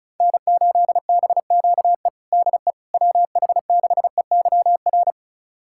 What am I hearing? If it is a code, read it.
Morse code, heard as a keyed tone, about 36 wpm, 700 Hz. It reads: N8BQE DE WH6EYR
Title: USA Callsign DE Callsign 35wpm